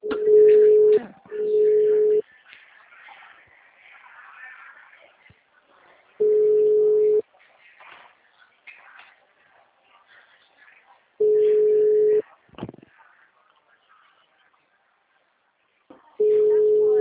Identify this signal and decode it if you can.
Ringback tone (Europe, 425 Hz), with speech.